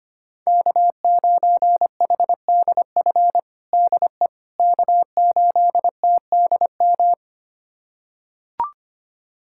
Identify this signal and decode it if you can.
Morse code, heard as a keyed tone, about 25 wpm, 700 Hz. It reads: K9HDF DE K8TDM
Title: USA Callsign DE Callsign - 5 Characters - 25wpm